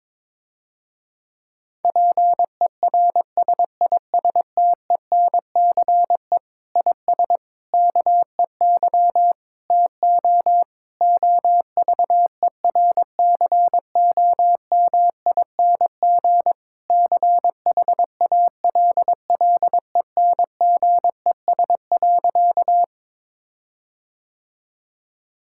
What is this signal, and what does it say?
Morse code, heard as a keyed tone, about 22 wpm, 700 Hz. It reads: PERSISTENCE IS KEY TO OVERCOMING CHALLENGES.
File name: Difficult Sentences with Hints - 22wpm